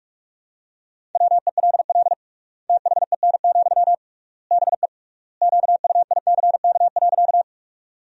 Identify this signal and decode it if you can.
Morse code, heard as a keyed tone, about 46 wpm, 700 Hz. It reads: WELL THEN, BE QUICK.
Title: 3x Word-Spacing - Sentences from Top 500 Words - 45wpm